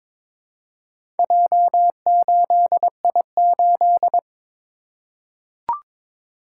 Morse code, heard as a keyed tone, about 22 wpm, 700 Hz. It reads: J8I8
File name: practice / 22wpm / Four Letter-Number Combinations - 22wpm